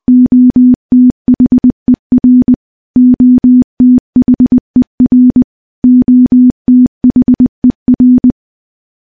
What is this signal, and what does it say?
Morse code, heard as a keyed tone, about 20 wpm, 250 Hz. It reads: OTHER OTHER OTHER
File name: morse-training-20WPM